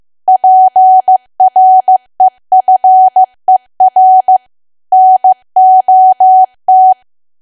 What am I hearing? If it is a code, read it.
Morse code, heard as a keyed tone, about 15 wpm, 750 Hz. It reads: PREFER NOT
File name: cwa / W1AW-Level2 / 15WPM